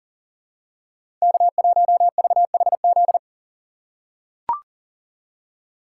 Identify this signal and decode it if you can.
Morse code, heard as a keyed tone, about 37 wpm, 700 Hz. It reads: K1VHZ
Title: USA Call Signs - 5 Characters - 2x Character Spacing - 40wpm